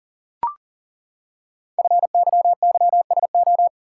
Morse code, heard as a keyed tone, about 41 wpm, 700 Hz. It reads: FYYSO
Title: Code Groups - 5 Letters - 40wpm